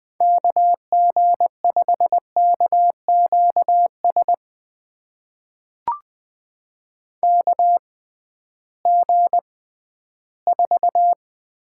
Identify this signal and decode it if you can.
Morse code, heard as a keyed tone, about 20 wpm, spaced as for 13 wpm, 700 Hz. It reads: KG5KQS KG4
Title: USA Call Signs - 6 Characters - 6x Character Spacing - 20wpm